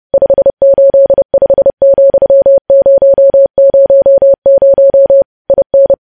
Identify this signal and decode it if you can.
Morse code, heard as a keyed tone, about 30 wpm, 550 Hz. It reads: 585,000 IN
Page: news headlines in morse code at 30 wpm